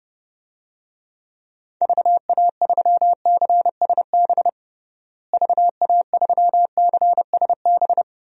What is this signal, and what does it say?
Morse code, heard as a keyed tone, about 32 wpm, 700 Hz. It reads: VA3CSB VA3CSB
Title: DXCC-100 Call Signs - 6 Characters - Repeated 3x - 30wpm